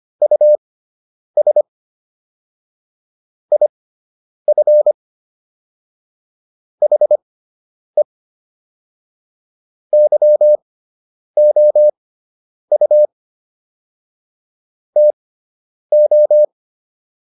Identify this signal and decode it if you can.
Morse code, heard as a keyed tone, about 25 wpm, spaced as for 9 wpm, 600 Hz. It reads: US IF HE YOU TO